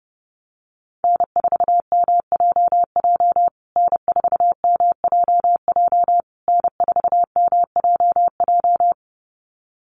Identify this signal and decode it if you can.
Morse code, heard as a keyed tone, about 30 wpm, 700 Hz. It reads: N4MJJ N4MJJ N4MJJ
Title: USA Call Signs Repeated 3x 30wpm